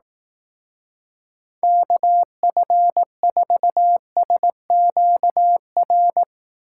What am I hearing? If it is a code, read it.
Morse code, heard as a keyed tone, about 18 wpm, 700 Hz. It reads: KF4SQR
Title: Speed Racing - USA 2x3 Call Signs 15wpm